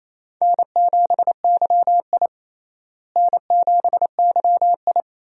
Morse code, heard as a keyed tone, about 28 wpm, 700 Hz. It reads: N7YI N7YI
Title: USA Call Signs - 4 Characters - Repeated 2x - 28wpm